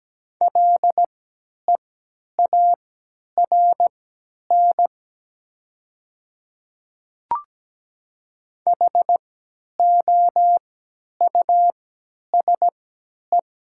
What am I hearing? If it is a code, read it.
Morse code, heard as a keyed tone, about 17 wpm, spaced as for 10 wpm, 690 Hz. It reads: LEARN HOUSE
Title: 3x Character Spacing - Top 500 Words - 17wpm